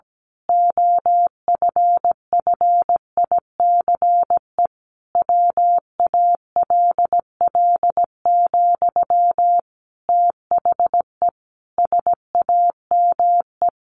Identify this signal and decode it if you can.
Morse code, heard as a keyed tone, about 17 wpm, 700 Hz. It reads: OFFICE WALL, THE SAME
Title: A Sound Of Thunder 17wpm